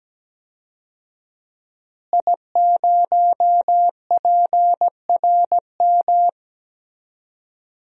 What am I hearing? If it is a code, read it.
Morse code, heard as a keyed tone, about 17 wpm, 700 Hz. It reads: I0PRM
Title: DXCC-100 Call Signs - 5 Characters - Repeated 3x - 17wpm